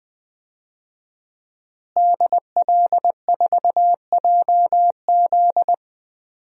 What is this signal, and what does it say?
Morse code, heard as a keyed tone, about 20 wpm, 700 Hz. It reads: DL4JZ
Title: DXCC-100 Call Signs - 5 Characters - Repeated 3x - 20wpm